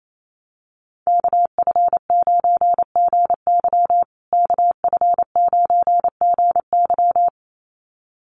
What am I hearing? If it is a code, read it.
Morse code, heard as a keyed tone, about 28 wpm, 700 Hz. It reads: KF9GY KF9GY
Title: USA Call Signs Repeated 2x 28wpm